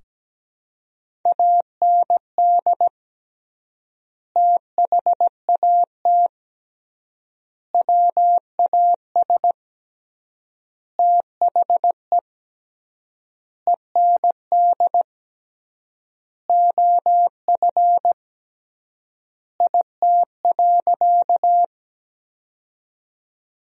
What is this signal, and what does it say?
Morse code, heard as a keyed tone, about 17 wpm, 700 Hz. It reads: AND THAT WAS THE END OF IT.
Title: 3x Word-Spacing - Sentences from Top 200 Words - 17wpm